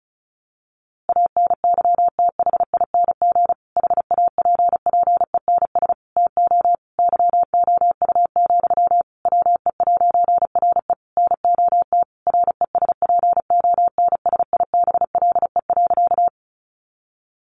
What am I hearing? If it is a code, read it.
Morse code, heard as a keyed tone, about 34 wpm, 700 Hz. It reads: ANYTHING HAPPENS TO YOU, WE'RE NOT RESPONSIBLE.
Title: A Sound Of Thunder 35wpm